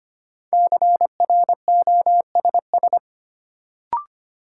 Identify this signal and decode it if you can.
Morse code, heard as a keyed tone, about 25 wpm, 700 Hz. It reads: CROSS
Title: Sentences from Top 300 Words - New Word Drill - 25wpm